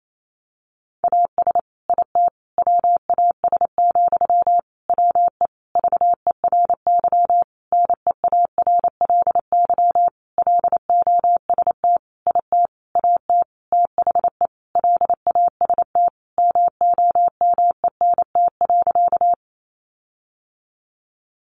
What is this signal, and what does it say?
Morse code, heard as a keyed tone, about 28 wpm, 700 Hz. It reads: AS IT WAS, WE VERY NEARLY LOST IT AT THE LAST MOMENT.
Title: Sentences from Top 1400 Words 28wpm